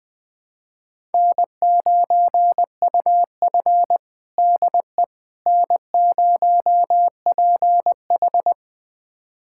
Morse code, heard as a keyed tone, about 20 wpm, 700 Hz. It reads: N9UF DE N0PH
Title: USA Callsign DE Callsign - 4 Characters - 20wpm